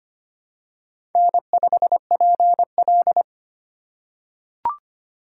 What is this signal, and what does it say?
Morse code, heard as a keyed tone, about 25 wpm, 700 Hz. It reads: N5PL
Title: USA Call Signs - 4 Characters - 4x Character Spacing - 25wpm